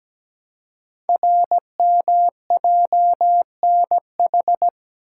Morse code, heard as a keyed tone, about 17 wpm, 700 Hz. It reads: RMJNH
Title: Code Groups - 5 Letters - 17wpm